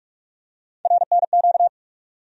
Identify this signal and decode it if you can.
Morse code, heard as a keyed tone, about 47 wpm, 700 Hz. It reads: RNQ
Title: USA Call Sign Postfixes - 3 Letters 45wpm